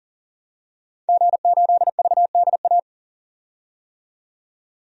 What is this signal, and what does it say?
Morse code, heard as a keyed tone, about 37 wpm, 700 Hz. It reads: G8VDA
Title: DXCC-100 Call Signs - 5 Characters - 40wpm